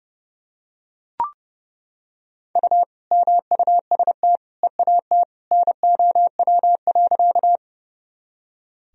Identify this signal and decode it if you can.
Morse code, heard as a keyed tone, about 30 wpm, 700 Hz. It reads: U MUST EAT NOW.
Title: Sentences from Top 300 Words and Abbreviations 30wpm